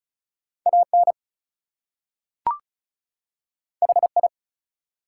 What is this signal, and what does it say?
Morse code, heard as a keyed tone, about 33 wpm, 700 Hz. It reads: AN HI
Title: Two Letter Words - 35wpm